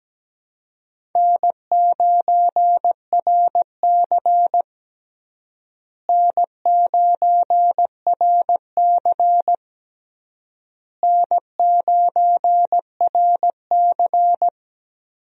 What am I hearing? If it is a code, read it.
Morse code, heard as a keyed tone, about 17 wpm, 700 Hz. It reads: N9RC N9RC N9RC
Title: USA Call Signs - 4 Characters - Repeated 3x - 17wpm